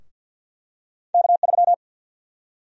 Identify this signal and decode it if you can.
Morse code, heard as a keyed tone, about 52 wpm, 700 Hz. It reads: K3
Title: USA Call Sign Prefixes - 50wpm